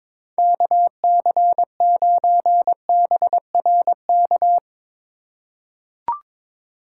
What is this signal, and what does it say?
Morse code, heard as a keyed tone, about 22 wpm, 700 Hz. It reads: KC9BRK
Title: USA Call Signs - 4x Character Spacing - 22wpm